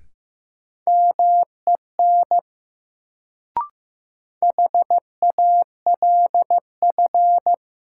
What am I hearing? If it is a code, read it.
Morse code, heard as a keyed tone, about 15 wpm, 700 Hz. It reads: MEN HALF
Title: Sets of 3 Words - New Word Drill - 15wpm